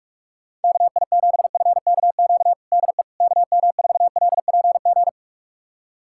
Morse code, heard as a keyed tone, about 46 wpm, 700 Hz. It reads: KI7FKQ DE KM4LPG
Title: USA Callsign DE Callsign - 6 Characters - 45wpm